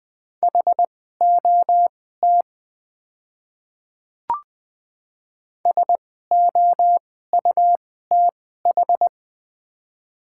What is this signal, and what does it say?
Morse code, heard as a keyed tone, about 20 wpm, spaced as for 15 wpm, 700 Hz. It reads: HOT SOUTH
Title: 2x Character Spacing - Top 500 Words - 20wpm